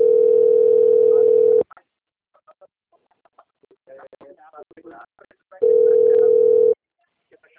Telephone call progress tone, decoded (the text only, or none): ringback tone